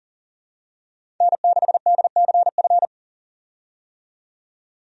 Morse code, heard as a keyed tone, about 40 wpm, 700 Hz. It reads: N6DCF
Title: USA Call Signs - 5 Characters - 40wpm